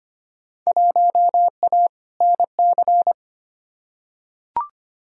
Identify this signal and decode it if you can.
Morse code, heard as a keyed tone, about 25 wpm, 700 Hz. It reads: 1A NC
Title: Field Day Exchanges Encore - 25wpm